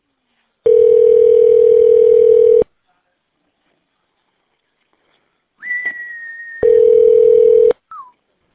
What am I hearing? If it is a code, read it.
Ringback tone (North America), 440 + 480 Hz.